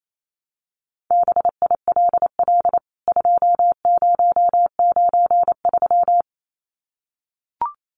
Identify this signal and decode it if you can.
Morse code, heard as a keyed tone, about 28 wpm, 700 Hz. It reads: BILL 2093
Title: CWT Exchange 28wpm